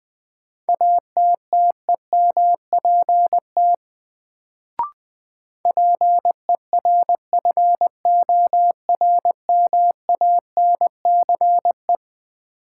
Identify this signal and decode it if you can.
Morse code, heard as a keyed tone, about 21 wpm, 700 Hz. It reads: ATTEMPT PERFORMANCE
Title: Sentences from Top 1000 Words - New Word Drill - 20wpm